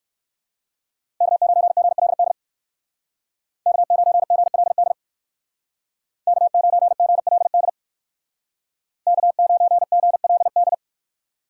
Morse code, heard as a keyed tone, about 60 wpm, 700 Hz. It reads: K9GLD K9GLD K9GLD K9GLD